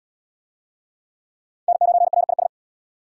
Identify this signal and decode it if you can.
Morse code, heard as a keyed tone, about 72 wpm, 700 Hz. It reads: N9RIS